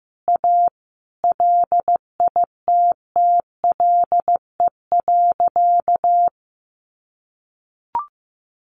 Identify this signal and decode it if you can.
Morse code, heard as a keyed tone, about 15 wpm, 700 Hz. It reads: A LITTLE.
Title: Sentences from Top 1500 Words 15wpm